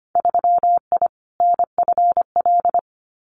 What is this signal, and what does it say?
Morse code, heard as a keyed tone, about 25 wpm, 700 Hz. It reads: 3I NFL
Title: Winter Field Day Exchanges 25wpm